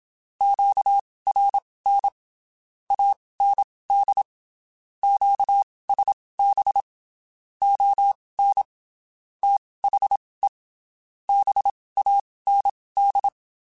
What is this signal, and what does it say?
Morse code, heard as a keyed tone, about 27 wpm, spaced as for 19 wpm, 800 Hz. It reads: QRN AND QSB ON THE BAND
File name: (spaced) / QSOs Spaced-mp3s / WD0BGP 1 @25 sp